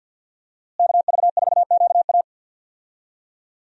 Morse code, heard as a keyed tone, about 46 wpm, 700 Hz. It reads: KV4QA
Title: USA Call Signs - 5 Characters - Repeated 3x - 50wpm